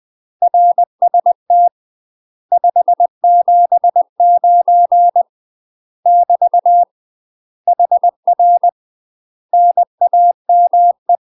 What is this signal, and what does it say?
Morse code, heard as a keyed tone, about 20 wpm, 700 Hz. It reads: RST 579 <BT> HR NAME
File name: UK MASTER QSOs TRAINING / 20WPM / UK MASTER QSOs 0006